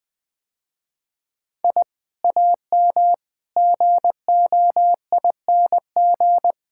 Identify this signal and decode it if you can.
Morse code, heard as a keyed tone, about 20 wpm, 700 Hz. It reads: I AM GOING
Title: Sentences with 2-Word Phrases - Spoken Beginning - 20wpm